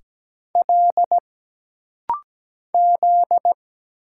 Morse code, heard as a keyed tone, about 17 wpm, 700 Hz. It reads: L Z
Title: Single Letters - Rapid-Fire 17wpm